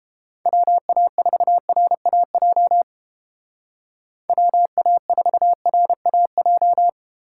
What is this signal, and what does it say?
Morse code, heard as a keyed tone, about 32 wpm, 700 Hz. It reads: WA4RAJ WA4RAJ